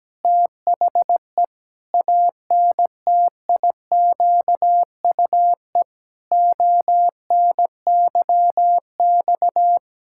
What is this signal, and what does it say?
Morse code, heard as a keyed tone, about 17 wpm, 700 Hz. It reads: THE ANTIQUE ONYX
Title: Pangram Challenge v2 - 17wpm